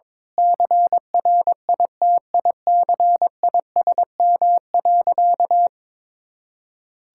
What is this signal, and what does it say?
Morse code, heard as a keyed tone, about 22 wpm, 700 Hz. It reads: CRITICISM.
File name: practice / 22wpm / Sentences with 8-Word Phrases - 22wpm